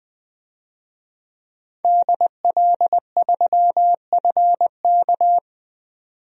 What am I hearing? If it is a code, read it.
Morse code, heard as a keyed tone, about 20 wpm, 700 Hz. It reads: DL3FK
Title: DXCC-100 Call Signs - 5 Characters - 20wpm